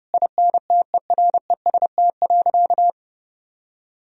Morse code, heard as a keyed tone, about 30 wpm, 700 Hz. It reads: INTEREST.
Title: Sentences from Top 1000 Words - Review Part 4 - 30wpm